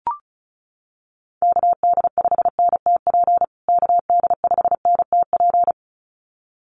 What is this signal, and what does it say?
Morse code, heard as a keyed tone, about 34 wpm, 700 Hz. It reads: KD5NTP KD5NTP
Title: USA Call Signs Repeated 2x 35wpm